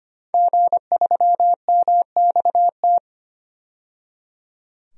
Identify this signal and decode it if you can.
Morse code, heard as a keyed tone, about 25 wpm, 700 Hz. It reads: G3MXT
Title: DXCC-100 Call Signs - 5 Characters - 25wpm